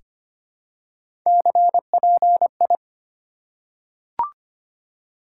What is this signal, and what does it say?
Morse code, heard as a keyed tone, about 25 wpm, 700 Hz. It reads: CPI
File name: Top 90 QSO Elements - 4x Character Spacing - 25wpm